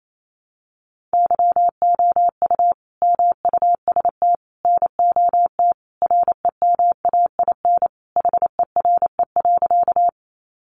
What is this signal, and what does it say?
Morse code, heard as a keyed tone, about 28 wpm, 700 Hz. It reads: YOU MUST NOT REMAIN HERE.
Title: Sentences from Top 1200 Words 28wpm